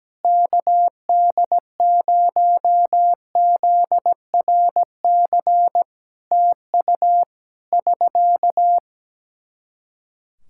Morse code, heard as a keyed tone, about 17 wpm, 700 Hz. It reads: KD0ZRC TU <SK>
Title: K1USN SST QSO Practice 17wpm